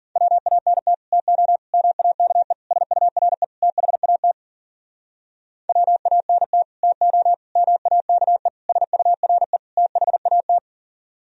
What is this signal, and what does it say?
Morse code, heard as a keyed tone, about 51 wpm, 700 Hz. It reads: WANT TO MAKE SURE THAT WANT TO MAKE SURE THAT